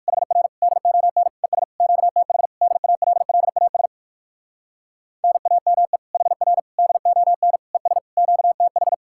Morse code, heard as a keyed tone, about 59 wpm, 700 Hz. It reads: HR DON ES QTH DALLAS NAME HR DON ES QTH